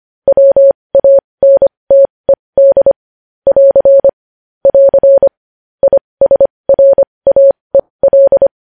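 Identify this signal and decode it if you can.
Morse code, heard as a keyed tone, about 25 wpm, spaced as for 20 wpm, 550 Hz. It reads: WANTED <AR> <AR> ISRAEL
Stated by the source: Morse code news headlines sent at 20 words per minute.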